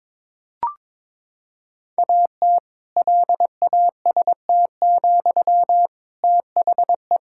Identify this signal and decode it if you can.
Morse code, heard as a keyed tone, about 22 wpm, 700 Hz. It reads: AT LAST, THE
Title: Sentences from Top 1000 Words - Review Part 5 - 22wpm